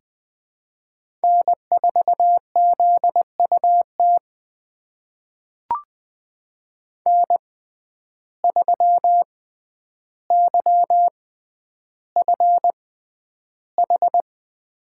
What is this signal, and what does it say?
Morse code, heard as a keyed tone, about 20 wpm, spaced as for 10 wpm, 700 Hz. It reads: N4ZUT N3YFH
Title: USA Call Signs - 5 Characters - 6x Character Spacing - 20wpm